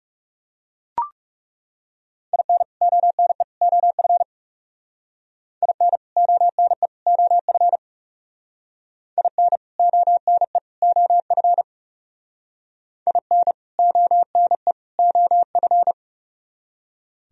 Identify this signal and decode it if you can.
Morse code, heard as a keyed tone, about 37 wpm, 700 Hz. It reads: IN ONE OF IN ONE OF IN ONE OF IN ONE OF
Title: Speed-Racing - Sets of 3 Words 30wpm